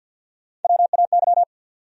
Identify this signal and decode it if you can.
Morse code, heard as a keyed tone, about 51 wpm, 700 Hz. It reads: WAY
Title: Top 100 Words Repeated 50wpm